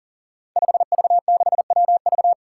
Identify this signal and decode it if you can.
Morse code, heard as a keyed tone, about 40 wpm, 700 Hz. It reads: 5V6WV
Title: Five Challenging Letter-Number Combinations 40wpm